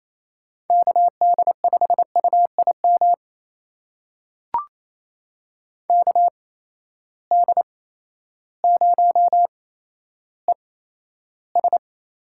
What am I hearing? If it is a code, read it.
Morse code, heard as a keyed tone, about 28 wpm, spaced as for 13 wpm, 700 Hz. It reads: KD5UIM KD0ES
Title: USA Call Signs - 6 Characters - 8x Character Spacing - 28wpm